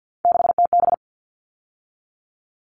Morse code, heard as a keyed tone, about 50 wpm, 700 Hz. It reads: 6TB
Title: All Cut Numbers - 3-Digit 50wpm